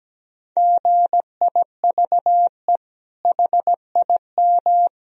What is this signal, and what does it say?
Morse code, heard as a keyed tone, about 17 wpm, 700 Hz. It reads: GIVE HIM
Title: Sentences from Top 1000 Words - Review Part 1 - 17wpm